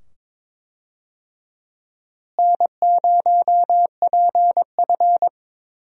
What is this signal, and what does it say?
Morse code, heard as a keyed tone, about 21 wpm, 700 Hz. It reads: N0PF
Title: USA Call Signs - 4 Characters - Repeated 2x - 22wpm